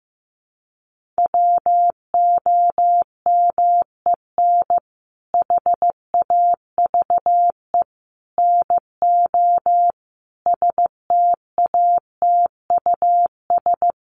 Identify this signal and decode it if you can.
Morse code, heard as a keyed tone, about 15 wpm, 700 Hz. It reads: WOMEN HAVE NO STATUS
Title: Sentences from Top 1400 Words 15wpm